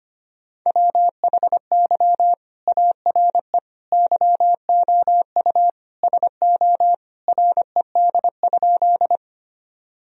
Morse code, heard as a keyed tone, about 25 wpm, 700 Hz. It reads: WHY ARE YOU SO RED?
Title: Sentences from Top 1000 Words - Review Part 2 - 25wpm